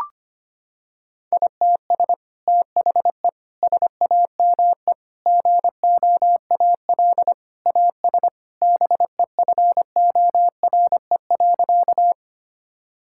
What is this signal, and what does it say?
Morse code, heard as a keyed tone, about 25 wpm, 700 Hz. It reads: ITS THE SAME GOAL AS BEFORE.
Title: Sentences from Top 1000 Words - Review Part 5 - 25wpm